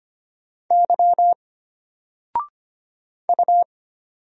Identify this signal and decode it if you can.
Morse code, heard as a keyed tone, about 25 wpm, 700 Hz. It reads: Y U
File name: Single-Letter-Number-25wpm